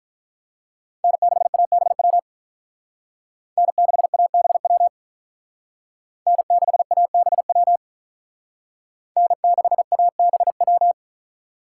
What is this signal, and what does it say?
Morse code, heard as a keyed tone, about 43 wpm, 700 Hz. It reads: N6ABW N6ABW N6ABW N6ABW
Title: Speed Racing - USA 1x3 Call Signs 35wpm